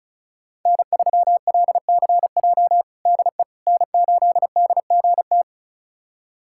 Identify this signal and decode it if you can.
Morse code, heard as a keyed tone, about 36 wpm, 700 Hz. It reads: N3LCJ DE N8DGT
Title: USA Callsign DE Callsign 35wpm